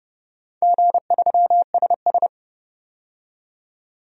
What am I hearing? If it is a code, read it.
Morse code, heard as a keyed tone, about 31 wpm, 700 Hz. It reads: G3SS
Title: DXCC-100 Call Signs - 4 Characters - 30wpm